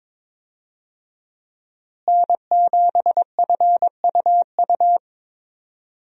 Morse code, heard as a keyed tone, about 22 wpm, 700 Hz. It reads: N7FUU
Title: USA Call Signs - 5 Characters - Repeated 3x - 22wpm